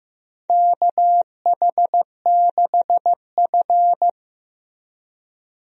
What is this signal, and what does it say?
Morse code, heard as a keyed tone, about 15 wpm, 700 Hz. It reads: KH6F
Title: USA Call Signs - 4 Characters - Repeated 2x - 15wpm